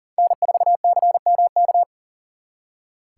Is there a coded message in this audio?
Yes: Morse code from 0.2 to 1.8 s, heard as a keyed tone, about 51 wpm, 700 Hz. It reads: N4CMK